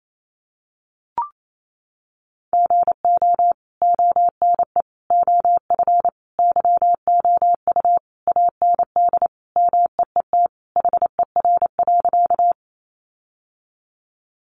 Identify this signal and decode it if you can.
Morse code, heard as a keyed tone, about 28 wpm, 700 Hz. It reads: GO ONE OF YOU AND MEET HER.
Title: Sentences from Top 1100 Words 28wpm